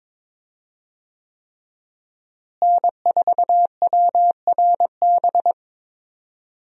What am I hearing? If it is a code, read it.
Morse code, heard as a keyed tone, about 22 wpm, 700 Hz. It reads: N4WRB
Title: USA Call Signs - 5 Characters - Repeated 3x - 22wpm